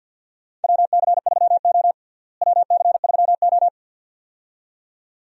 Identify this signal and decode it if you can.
Morse code, heard as a keyed tone, about 51 wpm, 700 Hz. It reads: WC3O WC3O
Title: USA Call Signs - 4 Characters - Repeated 2x - 50wpm